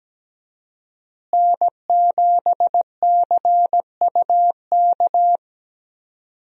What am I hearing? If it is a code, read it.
Morse code, heard as a keyed tone, about 17 wpm, 700 Hz. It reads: N7CUK
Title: USA Call Signs - 5 Characters - Repeated 3x - 17wpm